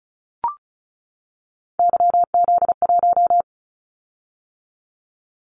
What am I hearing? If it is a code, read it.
Morse code, heard as a keyed tone, about 36 wpm, 700 Hz. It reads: YZ1
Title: Callsign Prefixes for Top 25 DXCC - 35wpm